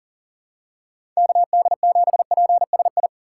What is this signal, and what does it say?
Morse code, heard as a keyed tone, about 40 wpm, 700 Hz. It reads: KD7PSI
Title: USA Call Signs - 6 Characters - 2x Character Spacing - 40wpm